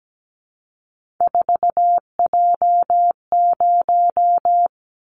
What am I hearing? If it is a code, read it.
Morse code, heard as a keyed tone, about 17 wpm, 700 Hz. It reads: 4J0
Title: Callsign Prefixes for Top 100 DXCC - 17wpm